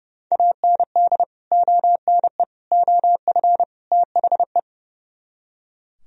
Morse code, heard as a keyed tone, about 37 wpm, 700 Hz. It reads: AND ONE OF THE
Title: Speed-Racing - Sets of 4 Words 30wpm